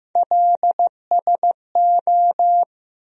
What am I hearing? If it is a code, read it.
Morse code, heard as a keyed tone, about 15 wpm, 700 Hz. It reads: LSO
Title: USA Call Sign Postfixes - 3 Letters 15wpm